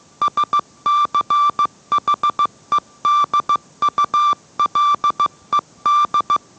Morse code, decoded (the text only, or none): SCHEDULED